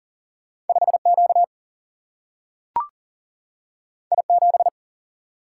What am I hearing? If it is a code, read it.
Morse code, heard as a keyed tone, about 40 wpm, 700 Hz. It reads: 5 Q I 7
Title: Single-Letter Single-Number 40wpm